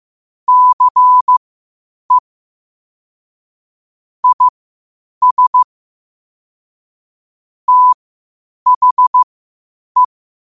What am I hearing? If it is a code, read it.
Morse code, heard as a keyed tone, about 15 wpm, spaced as for 8 wpm, 1000 Hz. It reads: CE IS THE
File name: qotdfiles / 8wpm